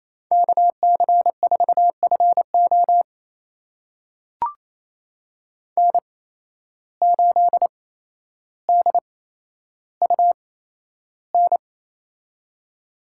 Morse code, heard as a keyed tone, about 28 wpm, spaced as for 12 wpm, 700 Hz. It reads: KC4FO N8DUN
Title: USA Call Signs - 5 Characters - 8x Character Spacing - 28wpm